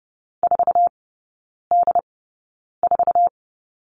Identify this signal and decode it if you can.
Morse code, heard as a keyed tone, about 30 wpm, spaced as for 9 wpm, 700 Hz. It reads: 4D4
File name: 081 Lesson 14e - Number 2 - Three Characters 30wpm